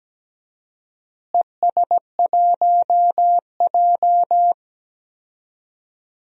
Morse code, heard as a keyed tone, about 17 wpm, 700 Hz. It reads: ES1J
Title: DXCC-100 Call Signs - 4 Characters - Repeated 3x - 17wpm